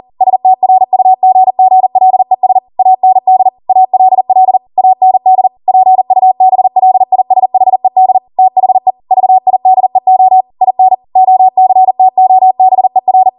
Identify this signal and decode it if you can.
Morse code, heard as a keyed tone, about 40 wpm, 750 Hz. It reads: STRUGGLES AND ALL AND PUBLISHED THE VIDEO IN OCTOBER